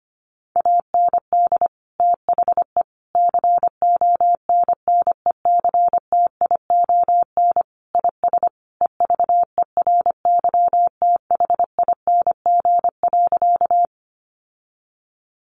Morse code, heard as a keyed tone, about 25 wpm, 700 Hz. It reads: AND THE CONNECTION IS EVERYTHING.
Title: Sentences from Top 1300 Words 25wpm